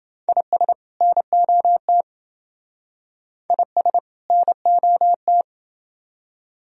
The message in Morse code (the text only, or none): IS NOT IS NOT